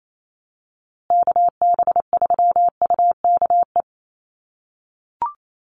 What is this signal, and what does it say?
Morse code, heard as a keyed tone, about 28 wpm, 700 Hz. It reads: KB3UKE
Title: USA Call Signs Repeated 2x 28wpm